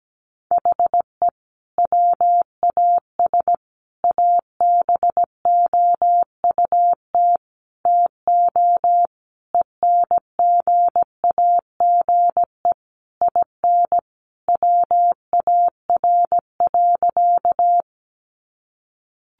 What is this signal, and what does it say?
Morse code, heard as a keyed tone, about 17 wpm, 700 Hz. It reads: HE WAS ABOUT TO ENGAGE IN WAR.
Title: Sentences from Top 1500 Words 17wpm